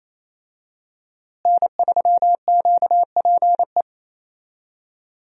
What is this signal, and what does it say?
Morse code, heard as a keyed tone, about 27 wpm, 700 Hz. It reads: N3QPE
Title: USA Call Signs - 5 Characters - Repeated 2x - 28wpm